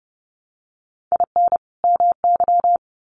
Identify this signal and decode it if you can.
Morse code, heard as a keyed tone, about 30 wpm, 700 Hz. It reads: IN MY
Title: Sets of 2 Words 30wpm